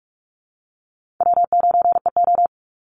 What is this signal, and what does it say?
Morse code, heard as a keyed tone, about 44 wpm, 700 Hz. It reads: W9EO
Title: Easy Call Signs Repeated 2x - 45wpm